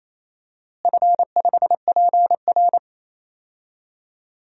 Morse code, heard as a keyed tone, about 28 wpm, 700 Hz. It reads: F5PR
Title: DXCC-100 Call Signs - 4 Characters - Repeated 3x - 28wpm